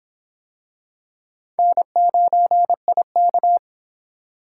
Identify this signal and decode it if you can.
Morse code, heard as a keyed tone, about 24 wpm, 700 Hz. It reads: N9IK